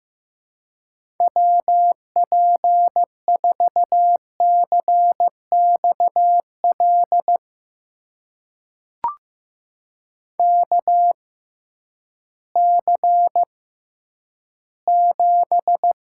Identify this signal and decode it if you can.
Morse code, heard as a keyed tone, about 16 wpm, spaced as for 10 wpm, 700 Hz. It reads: WP4CXL KC7
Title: USA Call Signs - 6 Characters - 6x Character Spacing - 15wpm